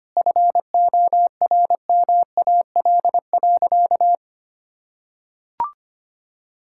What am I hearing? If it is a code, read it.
Morse code, heard as a keyed tone, about 25 wpm, 700 Hz. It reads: FORMAL.
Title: Sentences from Top 1600 Words 25wpm